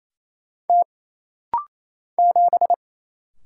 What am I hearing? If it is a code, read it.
Morse code, heard as a keyed tone, about 27 wpm, 700 Hz. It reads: T 7
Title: Single Letter-Number - Rapid-Fire 28wpm